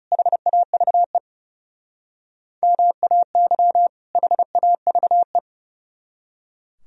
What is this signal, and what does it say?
Morse code, heard as a keyed tone, about 36 wpm, 700 Hz. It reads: HAVE MAY HAVE
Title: Speed-Racing - Sets of 2 Words 30wpm